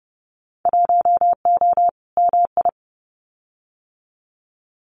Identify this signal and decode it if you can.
Morse code, heard as a keyed tone, about 30 wpm, 700 Hz. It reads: 1O MI
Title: Winter Field Day Exchanges Encore 30wpm